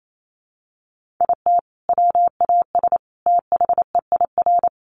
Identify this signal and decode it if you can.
Morse code, heard as a keyed tone, about 28 wpm, 700 Hz. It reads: IT WAS THEIR
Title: Sentences from Top 1500 Words 28wpm